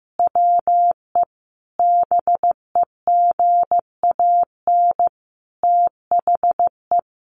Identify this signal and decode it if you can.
Morse code, heard as a keyed tone, about 15 wpm, 700 Hz. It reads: WE BEGAN THE
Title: Sentences from Top 1500 Words 15wpm